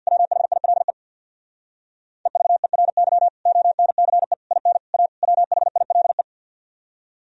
Morse code, heard as a keyed tone, about 64 wpm, 700 Hz. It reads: WHILE EVERY ONCE IN A WHILE